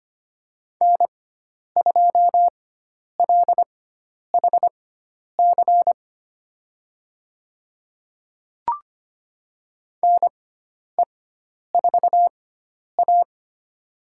Morse code, heard as a keyed tone, about 25 wpm, spaced as for 10 wpm, 700 Hz. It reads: N2LHC NE4A
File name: Lesson 16g - Letter c - Call Signs 25wpm